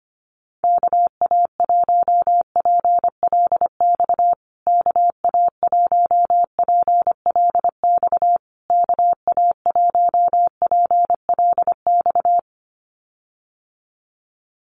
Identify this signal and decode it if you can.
Morse code, heard as a keyed tone, about 25 wpm, 700 Hz. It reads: KA1PLX KA1PLX KA1PLX
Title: USA Call Signs Repeated 3x 25wpm